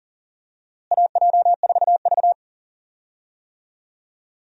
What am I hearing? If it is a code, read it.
Morse code, heard as a keyed tone, about 40 wpm, 700 Hz. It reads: AJ4V
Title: USA Call Signs - 4 Characters - Repeated 3x - 40wpm